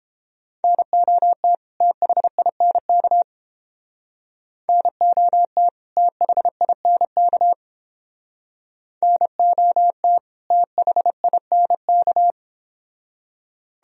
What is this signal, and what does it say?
Morse code, heard as a keyed tone, about 30 wpm, 700 Hz. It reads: NOT THINK NOT THINK NOT THINK